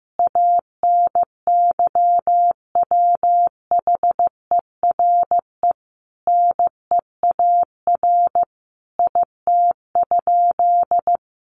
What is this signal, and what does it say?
Morse code, heard as a keyed tone, about 15 wpm, 700 Hz. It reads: ANYWHERE NEAR IT?
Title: Sentences from Top 1400 Words 15wpm